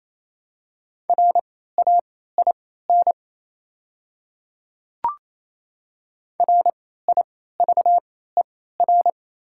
Morse code, heard as a keyed tone, about 28 wpm, spaced as for 16 wpm, 700 Hz. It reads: RAIN RIVER
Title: 3x Character Spacing - Top 300 Words - 28wpm